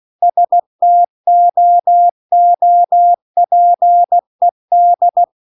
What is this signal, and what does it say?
Morse code, heard as a keyed tone, about 16 wpm, 700 Hz. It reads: STOOPED
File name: CW-The-Big-Twoheaded-River-2-16WPM